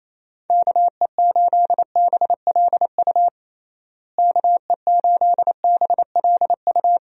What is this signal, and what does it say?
Morse code, heard as a keyed tone, about 27 wpm, 700 Hz. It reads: KE8BLU KE8BLU
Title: USA Call Signs - 6 Characters - Repeated 3x - 28wpm